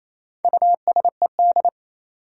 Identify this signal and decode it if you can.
Morse code, heard as a keyed tone, about 28 wpm, 700 Hz. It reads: USED
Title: Sentences from Top 600 Words - New Word Drill - 28wpm